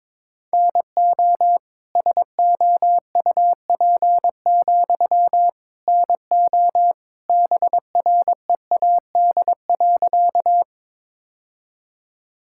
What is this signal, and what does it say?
Morse code, heard as a keyed tone, about 23 wpm, 700 Hz. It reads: NO SOUP, NO BREAD.
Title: Sentences from Top 1600 Words 22wpm